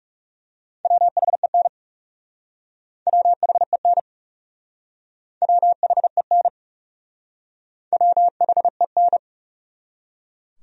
Morse code, heard as a keyed tone, about 40 wpm, 700 Hz. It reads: WHEN WHEN WHEN WHEN